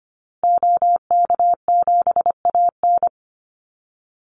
Morse code, heard as a keyed tone, about 25 wpm, 700 Hz. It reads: OK7AN
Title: Call Signs Repeated 3x - 25wpm